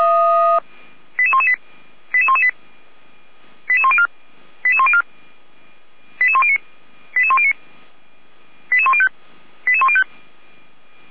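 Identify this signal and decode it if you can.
Five-tone selective call (ZVEI), heard as quick runs of five tones, tones of about 70 ms. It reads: 88118 88118 88104 88104 88109 88109 88105 88105
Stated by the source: Five-tone ZVEI selective call received on SDR.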